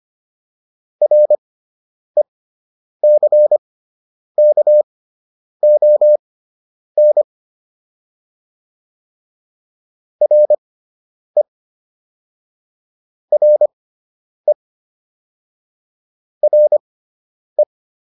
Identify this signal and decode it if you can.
Morse code, heard as a keyed tone, about 25 wpm, spaced as for 9 wpm, 600 Hz. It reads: RECKON RE RE RE